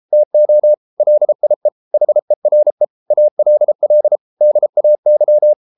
Morse code, heard as a keyed tone, about 33 wpm, 600 Hz. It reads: TO LIE HERE ALL DAY